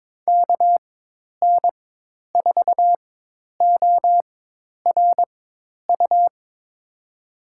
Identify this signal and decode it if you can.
Morse code, heard as a keyed tone, about 22 wpm, spaced as for 10 wpm, 700 Hz. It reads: KN4ORU
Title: USA Call Signs - 4x Character Spacing - 22wpm